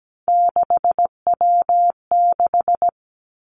Morse code, heard as a keyed tone, about 17 wpm, 700 Hz. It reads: 6W6
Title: Three Challenging Letter-Number Combinations 17wpm